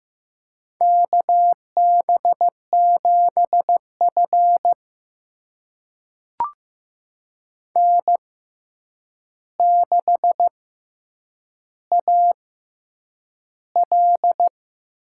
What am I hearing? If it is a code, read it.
Morse code, heard as a keyed tone, about 15 wpm, spaced as for 8 wpm, 700 Hz. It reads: KB7F N6AL
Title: USA Call Signs - 4 Characters - 6x Character Spacing - 15wpm